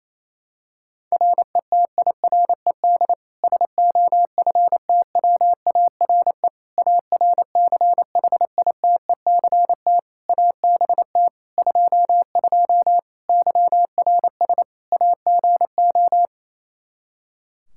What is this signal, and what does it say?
Morse code, heard as a keyed tone, about 28 wpm, 700 Hz. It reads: RETIRED SOFTWARE ARCHITECT ABT 22 YRS AGO
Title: Practice QSO - Exchange 3 - More Info - 28wpm